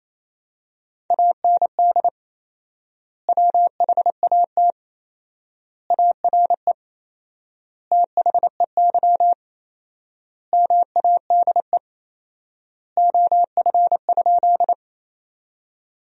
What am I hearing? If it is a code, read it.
Morse code, heard as a keyed tone, about 28 wpm, 700 Hz. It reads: AND WHAT ARE THEY MADE OF?
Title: 4x Word-Spacing - Sentences from Top 200 Words - 28wpm